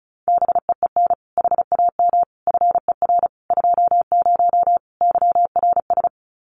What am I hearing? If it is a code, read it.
Morse code, heard as a keyed tone, about 34 wpm, 700 Hz. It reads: BEEN HAM FER 20 YRS
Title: Common QSO Phrases 35wpm